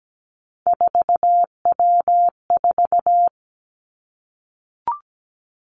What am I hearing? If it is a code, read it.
Morse code, heard as a keyed tone, about 17 wpm, 700 Hz. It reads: 4W4
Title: Three Challenging Letter-Number Combinations 17wpm